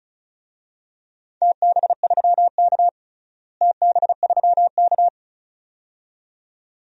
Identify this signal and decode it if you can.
Morse code, heard as a keyed tone, about 34 wpm, 700 Hz. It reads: TB3K TB3K
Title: DXCC-100 Call Signs - 4 Characters - Repeated 2x - 35wpm